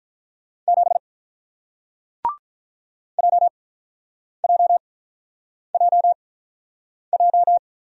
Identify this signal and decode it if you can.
Morse code, heard as a keyed tone, about 48 wpm, spaced as for 9 wpm, 700 Hz. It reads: 7 JJJJ